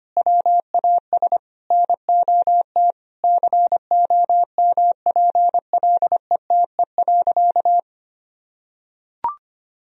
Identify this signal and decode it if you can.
Morse code, heard as a keyed tone, about 25 wpm, 700 Hz. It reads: WAS NOT COMPLETE.
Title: Sentences from Top 1000 Words - Review Part 2 - 25wpm